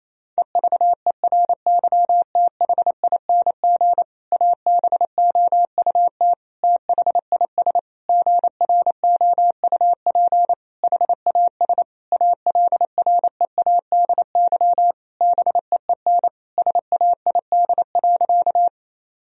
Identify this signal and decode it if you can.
Morse code, heard as a keyed tone, about 28 wpm, 700 Hz. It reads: EVERYTHING ABOUT THIS GROUP HAS ALREADY BEEN SAID.
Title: Sentences from Top 1000 Words - Review Part 1 - 28wpm